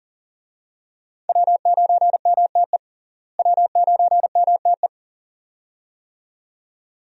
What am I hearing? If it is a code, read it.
Morse code, heard as a keyed tone, about 40 wpm, 700 Hz. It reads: W9MTE W9MTE
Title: USA Call Signs - 5 Characters - Repeated 2x - 40wpm